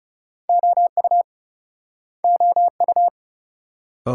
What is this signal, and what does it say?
Morse code, heard as a keyed tone, about 42 wpm, 700 Hz. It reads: OU OU